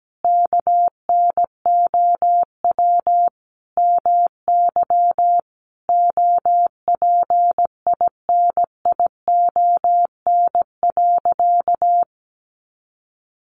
Morse code, heard as a keyed tone, about 17 wpm, 700 Hz. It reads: KNOW MY OPINION.
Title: Sentences from Top 1200 Words 17wpm